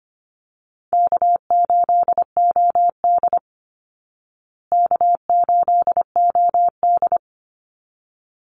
Morse code, heard as a keyed tone, about 25 wpm, 700 Hz. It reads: K8OD K8OD
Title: Easy Call Signs Repeated 2x - 25wpm